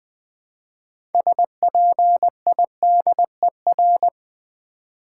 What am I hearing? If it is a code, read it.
Morse code, heard as a keyed tone, about 20 wpm, 700 Hz. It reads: SPIDER
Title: Build A Word - 3 Letters At A Time - 6-Letter Words - 20wpm